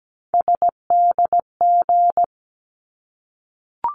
Morse code, heard as a keyed tone, about 17 wpm, 700 Hz. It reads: SDG
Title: ARRL Sections 17wpm